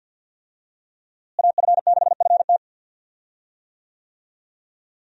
Morse code, heard as a keyed tone, about 50 wpm, 700 Hz. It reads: AF6FT
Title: USA Call Signs - 5 Characters - Repeated 3x - 50wpm